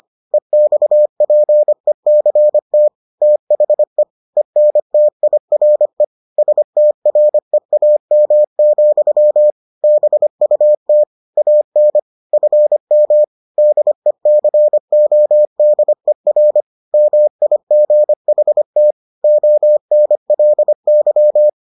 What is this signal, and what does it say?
Morse code, heard as a keyed tone, about 25 wpm, 600 Hz. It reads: EXPECT THE ENTIRE STREAM, BUT AN FM DECODER MIGHT ONLY